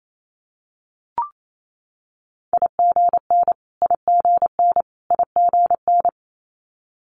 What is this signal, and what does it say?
Morse code, heard as a keyed tone, about 28 wpm, 700 Hz. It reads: IGN IGN IGN
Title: Multi-letter Phonemes Repeated 3x 28wpm